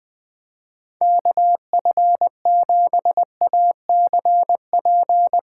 Morse code, heard as a keyed tone, about 22 wpm, 700 Hz. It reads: KF7ACP
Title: Speed Racing - USA 2x3 Call Signs 20wpm